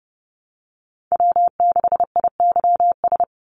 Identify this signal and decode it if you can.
Morse code, heard as a keyed tone, about 30 wpm, 700 Hz. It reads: W6IYS
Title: Call Signs - 30wpm